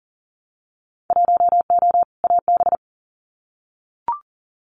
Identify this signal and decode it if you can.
Morse code, heard as a keyed tone, about 42 wpm, 700 Hz. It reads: 1O AB
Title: Winter Field Day Exchanges Encore 40wpm